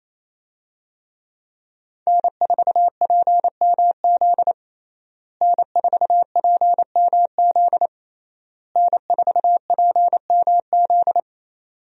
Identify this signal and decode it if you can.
Morse code, heard as a keyed tone, about 28 wpm, 700 Hz. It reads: N4PMZ N4PMZ N4PMZ
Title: USA Call Signs - 5 Characters - Repeated 3x - 28wpm